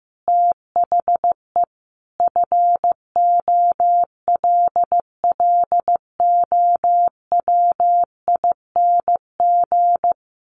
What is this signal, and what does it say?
Morse code, heard as a keyed tone, about 15 wpm, 700 Hz. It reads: THE FOLLOWING
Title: Sentences from Top 900 Words 15wpm